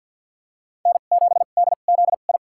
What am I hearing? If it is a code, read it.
Morse code, heard as a keyed tone, about 53 wpm, spaced as for 37 wpm, 690 Hz. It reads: N7DZI
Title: USA Call Signs - 5 Characters - 2x Character Spacing - 50wpm